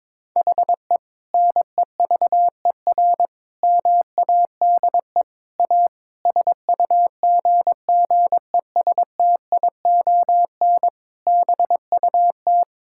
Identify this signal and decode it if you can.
Morse code, heard as a keyed tone, about 21 wpm, 700 Hz. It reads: HE NEVER MADE A SUGGESTION BUT
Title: Sentences from Top 1600 Words 22wpm